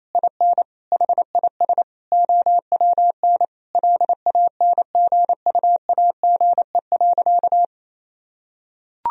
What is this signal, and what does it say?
Morse code, heard as a keyed tone, about 28 wpm, 700 Hz. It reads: IN HIS OWN LANGUAGE.
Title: Sentences from Top 1000 Words - Review Part 3 - 28wpm